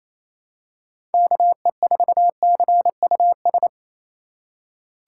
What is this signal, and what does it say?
Morse code, heard as a keyed tone, about 28 wpm, 700 Hz. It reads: KE4CUS
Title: USA Call Signs - 6 Characters - 4x Character Spacing - 28wpm